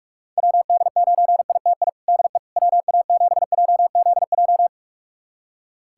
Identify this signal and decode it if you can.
Morse code, heard as a keyed tone, about 46 wpm, 700 Hz. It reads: WD9ITI DE WA7JZJ
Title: USA Callsign DE Callsign - 6 Characters - 45wpm